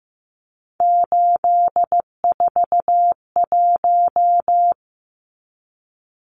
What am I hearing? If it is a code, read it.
Morse code, heard as a keyed tone, about 15 wpm, 700 Hz. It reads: 841
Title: Numbers - 3-Digit 15wpm